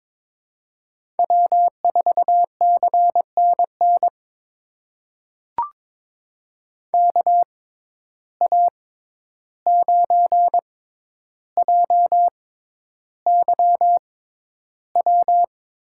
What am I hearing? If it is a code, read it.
Morse code, heard as a keyed tone, about 22 wpm, spaced as for 11 wpm, 700 Hz. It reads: W4CNN KA9JYW
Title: USA Call Signs - 6x Character Spacing - 22wpm